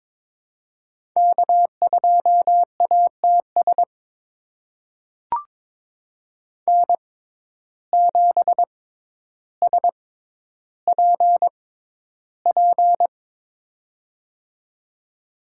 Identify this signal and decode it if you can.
Morse code, heard as a keyed tone, about 22 wpm, spaced as for 11 wpm, 700 Hz. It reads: K2ATS N7SPP
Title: USA Call Signs - 5 Characters - 6x Character Spacing - 22wpm